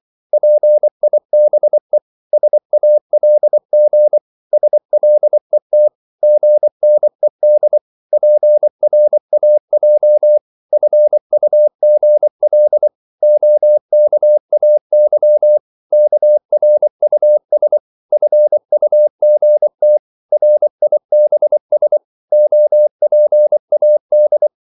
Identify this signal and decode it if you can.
Morse code, heard as a keyed tone, about 24 wpm, 600 Hz. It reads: PIBE SALG SLET GNED PRAJ FUGL OKAY KRUS FUGT RIBS OPAD
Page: Korte danske ord 24wpm